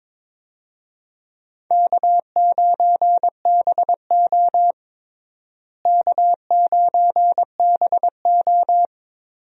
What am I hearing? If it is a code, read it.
Morse code, heard as a keyed tone, about 21 wpm, 700 Hz. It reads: K9BO K9BO
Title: USA Call Signs - 4 Characters - Repeated 2x - 22wpm